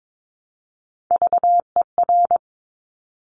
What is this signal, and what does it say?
Morse code, heard as a keyed tone, about 22 wpm, 700 Hz. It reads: VER
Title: Top Three Letter Combinations in Words 22wpm